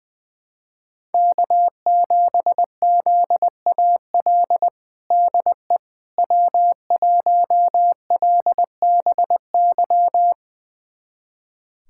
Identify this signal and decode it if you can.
Morse code, heard as a keyed tone, about 21 wpm, 700 Hz. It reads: K7ZAL DE W1LBY
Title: USA Callsign DE Callsign - 5 Characters - 20wpm